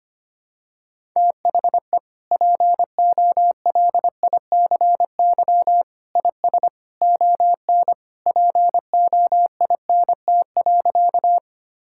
Morse code, heard as a keyed tone, about 25 wpm, 700 Hz. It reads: THE POLICY IS ON POINT.
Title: Sentences from Top 1000 Words - Review Part 5 - 25wpm